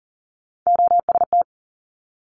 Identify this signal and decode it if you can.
Morse code, heard as a keyed tone, about 42 wpm, 700 Hz. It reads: OST
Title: Multi-letter Phonemes 40wpm